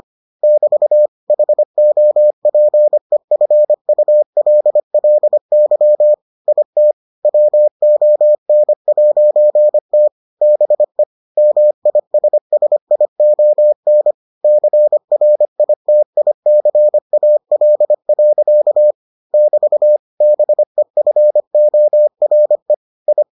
Morse code, heard as a keyed tone, about 25 wpm, 600 Hz. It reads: <BT> HOPEFULLY IT WON'T BE MISSION CRITICAL. <BT> BEFORE I